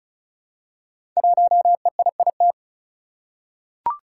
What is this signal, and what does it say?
Morse code, heard as a keyed tone, about 36 wpm, 700 Hz. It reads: 1EIIT
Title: Five Challenging Letter-Number Combinations 35wpm